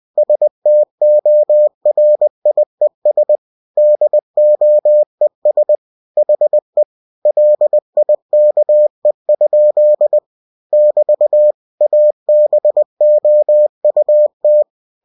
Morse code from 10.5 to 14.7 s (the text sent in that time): <BT> ABOUT